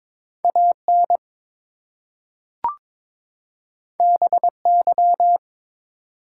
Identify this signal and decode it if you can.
Morse code, heard as a keyed tone, about 22 wpm, 700 Hz. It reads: AN BY
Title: Two Letter Words - 22wpm